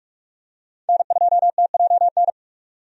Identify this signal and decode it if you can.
Morse code, heard as a keyed tone, about 46 wpm, 700 Hz. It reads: N2TJN
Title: USA Call Signs - 5 Characters - 45wpm